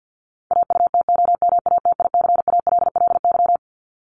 Morse code, heard as a keyed tone, about 49 wpm, 700 Hz. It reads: AUTOMATICALLY
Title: Top 1751-2000 Words 50wpm